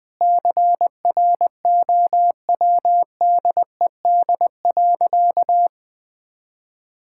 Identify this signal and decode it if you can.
Morse code, heard as a keyed tone, about 20 wpm, 700 Hz. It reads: CROWDED.
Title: Sentences with 7-Word Phrases - Spoken Beginning - 20wpm